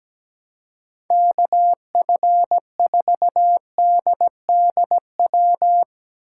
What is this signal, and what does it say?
Morse code, heard as a keyed tone, about 17 wpm, 700 Hz. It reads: KF4DDW
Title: USA Call Signs - 6 Characters - 6x Character Spacing - 17wpm